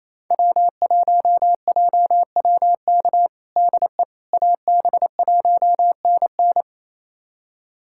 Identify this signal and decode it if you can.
Morse code, heard as a keyed tone, about 28 wpm, 700 Hz. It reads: W1JWK DE AB1NN
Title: USA Callsign DE Callsign - 5 Characters - 28wpm